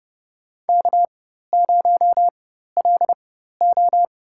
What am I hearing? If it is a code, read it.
Morse code, heard as a keyed tone, about 30 wpm, spaced as for 14 wpm, 700 Hz. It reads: K0LO
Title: USA Call Signs - 4 Characters - 4x Character Spacing - 30wpm